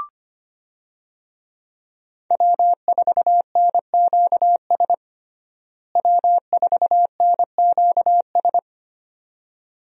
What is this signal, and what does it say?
Morse code, heard as a keyed tone, about 25 wpm, 700 Hz. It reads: W4NQS W4NQS
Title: USA Call Signs - 5 Characters - Repeated 2x - 25wpm